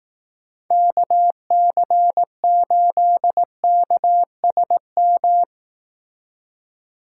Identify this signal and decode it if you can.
Morse code, heard as a keyed tone, about 18 wpm, 700 Hz. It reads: KC8KSM
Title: Speed Racing - USA Call Signs 15wpm